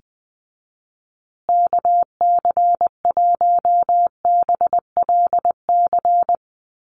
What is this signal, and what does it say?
Morse code, heard as a keyed tone, about 19 wpm, 700 Hz. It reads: KC1BLC
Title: USA Call Signs Repeated 3x 20wpm